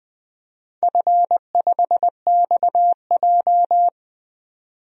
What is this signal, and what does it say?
Morse code, heard as a keyed tone, about 20 wpm, 700 Hz. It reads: F5XJ
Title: DXCC-100 Call Signs - 4 Characters - 20wpm